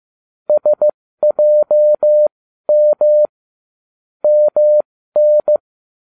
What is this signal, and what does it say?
Morse code, heard as a keyed tone, about 15 wpm, spaced as for 12 wpm, 600 Hz. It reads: SJM MN